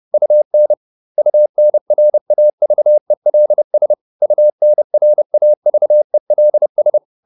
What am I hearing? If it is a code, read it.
Morse code, heard as a keyed tone, about 30 wpm, 600 Hz. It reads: UN UNRAVELS UNRAVELS